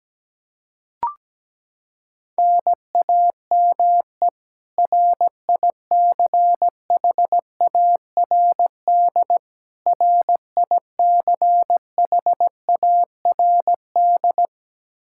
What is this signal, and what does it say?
Morse code, heard as a keyed tone, about 17 wpm, 700 Hz. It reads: NAME RICHARD RICHARD
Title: Practice QSO - Exchange 1 - Beginning - 17wpm